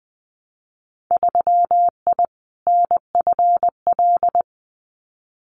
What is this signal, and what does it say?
Morse code, heard as a keyed tone, about 20 wpm, 700 Hz. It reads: 3I NFL
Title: Winter Field Day Exchanges 20wpm